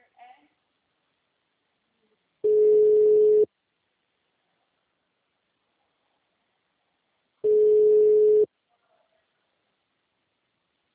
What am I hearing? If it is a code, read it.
Ringback tone (Europe, 425 Hz), with speech.
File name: out- -2006- - - .12